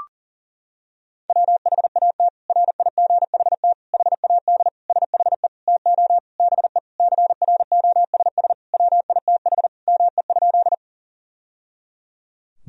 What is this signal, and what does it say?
Morse code, heard as a keyed tone, about 37 wpm, 700 Hz. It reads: WHAT RIGHT HAD SHE TO BE CROSS WITH ME?
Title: Sentences from Top 1000 Words - Review Part 1 - 40wpm